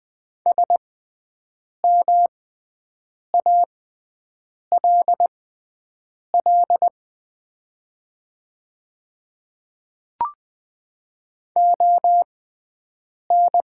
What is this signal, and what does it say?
Morse code, heard as a keyed tone, about 20 wpm, spaced as for 7 wpm, 700 Hz. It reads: SMALL ON
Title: 6x Character Spacing - Top 200 Words - 20wpm